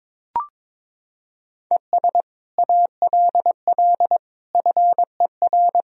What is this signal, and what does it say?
Morse code, heard as a keyed tone, about 22 wpm, 700 Hz. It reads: ES ALL FER
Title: Sentences from Top 100 Words and Abbreviations 22wpm